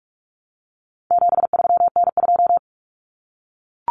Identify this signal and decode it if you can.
Morse code, heard as a keyed tone, about 44 wpm, 700 Hz. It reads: 73N2
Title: Common Cut Numbers - 4-Digit 45wpm